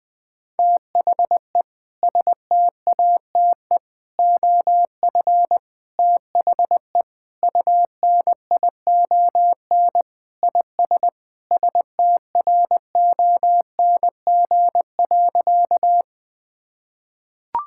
Morse code, heard as a keyed tone, about 21 wpm, 700 Hz. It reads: THE STATE OF THE UNION IS STRONG.
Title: Sentences with 3-Word Phrases - 20wpm